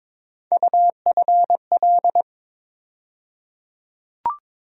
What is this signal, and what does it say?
Morse code, heard as a keyed tone, about 21 wpm, 700 Hz. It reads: UFL
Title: USA Call Sign Postfixes - 3 Letters 22wpm